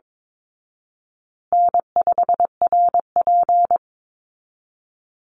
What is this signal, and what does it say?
Morse code, heard as a keyed tone, about 22 wpm, 700 Hz. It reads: N5RP
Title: Call Signs Repeated 3x - 22wpm